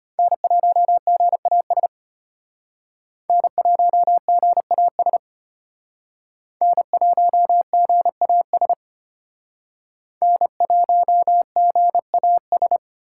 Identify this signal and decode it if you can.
Morse code, heard as a keyed tone, about 32 wpm, 700 Hz. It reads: N1GAS N1GAS N1GAS N1GAS